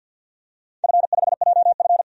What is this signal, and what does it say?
Morse code, heard as a keyed tone, about 51 wpm, 700 Hz. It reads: F5JF
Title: DXCC-100 Call Signs - 4 Characters - 50wpm